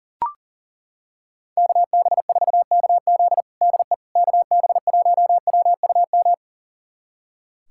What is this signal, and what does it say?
Morse code, heard as a keyed tone, about 40 wpm, 700 Hz. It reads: KB4KZ DE KB1WUM
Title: USA Callsign DE Callsign 40wpm